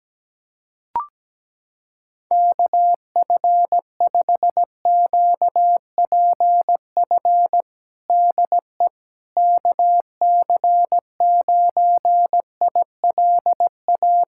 Morse code, heard as a keyed tone, about 17 wpm, 700 Hz. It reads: KF5QPF DE KC9ILA
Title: USA Callsign DE Callsign 17wpm